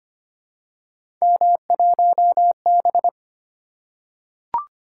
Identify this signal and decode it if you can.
Morse code, heard as a keyed tone, about 25 wpm, 700 Hz. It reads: M1B
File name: Three Letter-Number Combinations - 25wpm